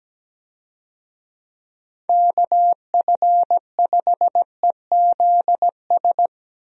Morse code, heard as a keyed tone, about 17 wpm, 700 Hz. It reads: KF5EZS
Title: USA Call Signs - 6 Characters - Repeated 3x - 17wpm